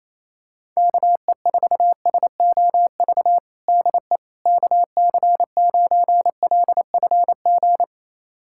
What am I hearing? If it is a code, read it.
Morse code, heard as a keyed tone, about 28 wpm, 700 Hz. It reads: KE4SOV DE KC9LFG
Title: USA Callsign DE Callsign - 6 Characters - 28wpm